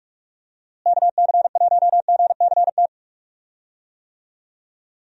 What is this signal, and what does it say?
Morse code, heard as a keyed tone, about 45 wpm, 700 Hz. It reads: KC1GCT
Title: USA Call Signs - 6 Characters - Repeated 3x - 45wpm